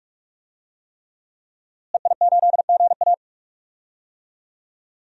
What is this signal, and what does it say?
Morse code, heard as a keyed tone, about 43 wpm, 700 Hz. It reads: EI8GA
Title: DXCC-100 Call Signs - 5 Characters - Repeated 3x - 45wpm